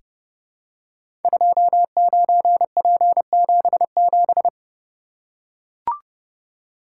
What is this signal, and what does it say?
Morse code, heard as a keyed tone, about 29 wpm, 700 Hz. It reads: 29P77
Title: Code Groups - 5 Letter:Numbers - 30wpm